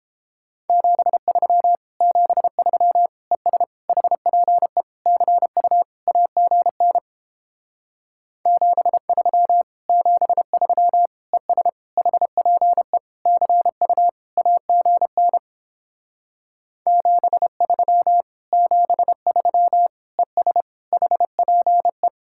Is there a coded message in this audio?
Yes: Morse code from 0.7 to 22.1 s, heard as a keyed tone, about 31 wpm, 700 Hz. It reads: 73 73 ES HPE CU AGN 73 73 ES HPE CU AGN 73 73 ES HPE